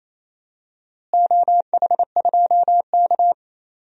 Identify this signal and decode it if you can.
Morse code, heard as a keyed tone, about 28 wpm, 700 Hz. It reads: OH2K
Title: DXCC-100 Call Signs - 4 Characters - Repeated 2x - 28wpm